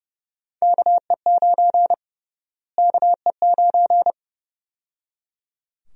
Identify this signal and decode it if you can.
Morse code, heard as a keyed tone, about 30 wpm, 700 Hz. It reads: KE9 KE9
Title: USA Call Sign Prefixes - Repeated 2x - 30wpm